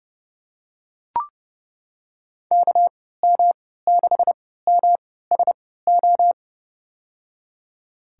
Morse code, heard as a keyed tone, about 30 wpm, spaced as for 17 wpm, 700 Hz. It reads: KM6MSO
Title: USA Call Signs - 6 Characters - 3x Character Spacing - 30wpm